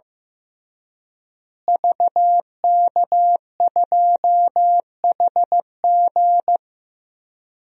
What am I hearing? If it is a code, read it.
Morse code, heard as a keyed tone, about 15 wpm, 700 Hz. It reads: VK2HG
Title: DXCC-100 Call Signs - 5 Characters - Repeated 3x - 15wpm